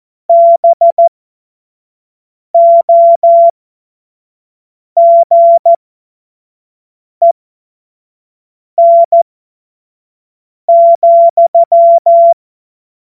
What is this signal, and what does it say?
Morse code, heard as a keyed tone, about 14 wpm, spaced as for 5 wpm, 680 Hz. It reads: BOGEN,
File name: Morsekurs / Witze / Witz 15. August 2023 25 F70